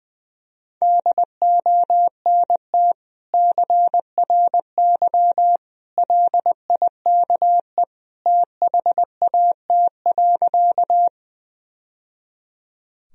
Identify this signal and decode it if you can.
Morse code, heard as a keyed tone, about 20 wpm, 700 Hz. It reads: DONT CRY LIKE THAT.
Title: Sentences from Top 1000 Words - Review Part 2 - 20wpm